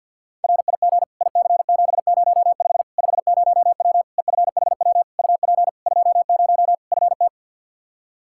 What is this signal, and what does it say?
Morse code, heard as a keyed tone, about 49 wpm, 700 Hz. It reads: RIG IC705 50W EFHW UP 20 FT
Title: Practice QSO - Exchange 2 - Info - 50wpm